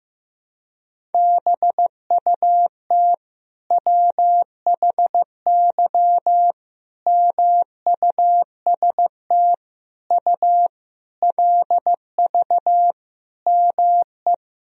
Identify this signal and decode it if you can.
Morse code, heard as a keyed tone, about 15 wpm, 700 Hz. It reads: BUT WHY MUST U LV ME
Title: Sentences from Top 400 Words and Abbreviations 15wpm